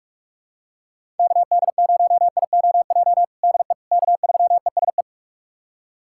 Morse code, heard as a keyed tone, about 44 wpm, 700 Hz. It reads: KD0IOJ DE K3ESE
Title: USA Callsign DE Callsign 45wpm